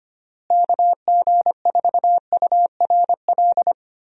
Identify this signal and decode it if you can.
Morse code, heard as a keyed tone, about 25 wpm, 700 Hz. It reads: KG4URL
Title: USA Call Signs - 6 Characters - Repeated 2x - 25wpm